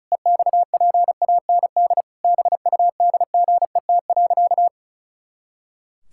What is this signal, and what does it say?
Morse code, heard as a keyed tone, about 36 wpm, 700 Hz. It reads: EXPAND BUDGET.
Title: Pangram Challenge v2 - 35wpm